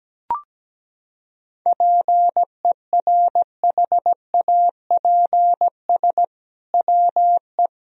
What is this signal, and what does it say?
Morse code, heard as a keyed tone, about 17 wpm, 700 Hz. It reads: PERHAPS WE
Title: Sentences from Top 1000 Words - Review Part 2 - 17wpm